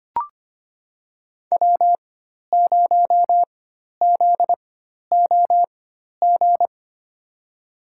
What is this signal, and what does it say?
Morse code, heard as a keyed tone, about 25 wpm, spaced as for 12 wpm, 700 Hz. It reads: W0ZOG
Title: USA Call Signs - 5 Characters - 4x Character Spacing - 25wpm